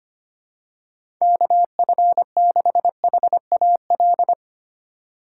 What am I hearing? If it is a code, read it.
Morse code, heard as a keyed tone, about 25 wpm, 700 Hz. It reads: KF6HAL
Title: USA Call Signs - 6 Characters - 2x Character Spacing - 25wpm